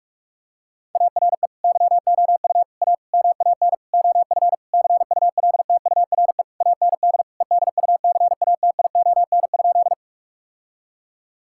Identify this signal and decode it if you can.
Morse code, heard as a keyed tone, about 46 wpm, 700 Hz. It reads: ARE YOU A MAN OF CULTURE AND EDUCATION?
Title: Sentences from Top 1000 Words - Review Part 6 - 45wpm